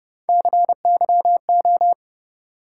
Morse code, heard as a keyed tone, about 29 wpm, 700 Hz. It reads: CYO